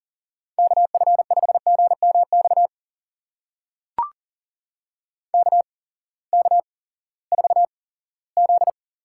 Morse code, heard as a keyed tone, about 37 wpm, spaced as for 20 wpm, 700 Hz. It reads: KF5GMX KK4Z
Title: USA Call Signs - 6 Characters - 8x Character Spacing - 40wpm